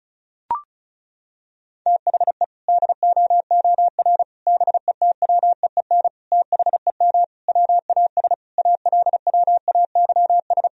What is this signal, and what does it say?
Morse code, heard as a keyed tone, about 36 wpm, 700 Hz. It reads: THE DOOR BETWEEN THEM WAS ALWAYS
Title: Sentences from Top 1000 Words - Review Part 1 - 35wpm